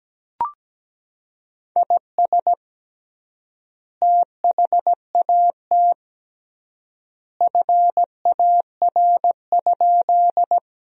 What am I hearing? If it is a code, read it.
Morse code, heard as a keyed tone, about 17 wpm, 700 Hz. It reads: IS THAT FAR?
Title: 3x Word-Spacing - Sentences from Top 300 Words - 17wpm